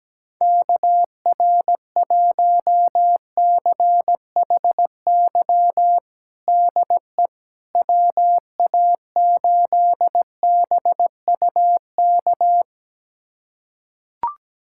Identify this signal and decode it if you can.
Morse code, heard as a keyed tone, about 17 wpm, 700 Hz. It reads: KR1CHY DE WA8BUK
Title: USA Callsign DE Callsign - 6 Characters - 17wpm